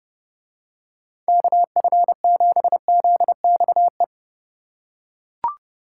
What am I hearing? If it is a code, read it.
Morse code, heard as a keyed tone, about 29 wpm, 700 Hz. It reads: KF7ZXE